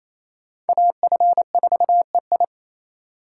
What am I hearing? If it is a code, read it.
Morse code, heard as a keyed tone, about 28 wpm, 700 Hz. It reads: AF4EI
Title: USA Call Signs - 5 Characters - 2x Character Spacing - 28wpm